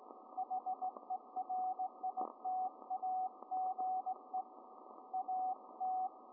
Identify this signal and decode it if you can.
Morse code, heard as a keyed tone, about 18 wpm, 750 Hz, with radio noise. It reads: HERITAGE AT